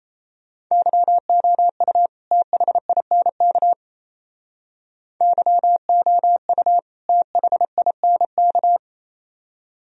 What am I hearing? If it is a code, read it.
Morse code, heard as a keyed tone, about 34 wpm, 700 Hz. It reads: YOU THINK YOU THINK
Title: Speed-Racing - Sets of 2 Words 28wpm